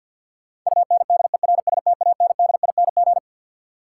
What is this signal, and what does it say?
Morse code, heard as a keyed tone, about 52 wpm, 700 Hz. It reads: UNDERSTANDING
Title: Sentences from Top 1000 Words - Review Part 3 - New Word Drill - 50wpm